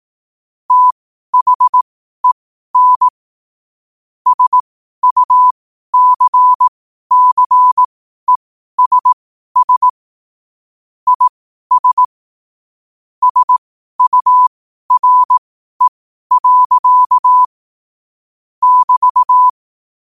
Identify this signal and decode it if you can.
Morse code, heard as a keyed tone, about 18 wpm, spaced as for 12 wpm, 1000 Hz. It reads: THEN SUCCESS IS SURE. <BT>
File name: qotdfiles / 12wpm